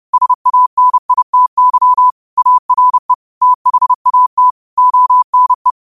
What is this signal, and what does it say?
Morse code, heard as a keyed tone, about 30 wpm, 1000 Hz. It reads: SANITY ARE THAT ONE